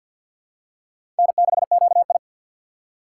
Morse code, heard as a keyed tone, about 47 wpm, 700 Hz. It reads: N6QI
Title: USA Call Signs - 6x Character Spacing - 50wpm